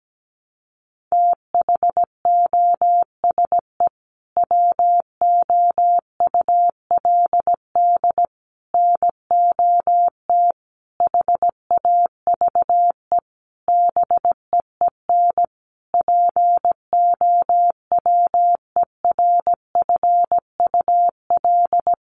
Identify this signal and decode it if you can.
Morse code, heard as a keyed tone, about 17 wpm, 700 Hz. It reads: THOSE WOULD NOT HAVE BEEN POWERFUL
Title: Sentences from Top 1300 Words 17wpm